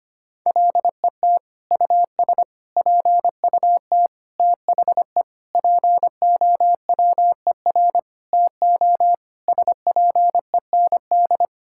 Morse code, heard as a keyed tone, about 25 wpm, 700 Hz. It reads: LET US PUT THE POWER TO SPEND
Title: Sentences from Top 1000 Words - Review Part 5 - 25wpm